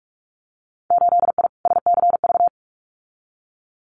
Morse code, heard as a keyed tone, about 46 wpm, 690 Hz. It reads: 8I SCV
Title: Winter Field Day Exchanges Encore 45wpm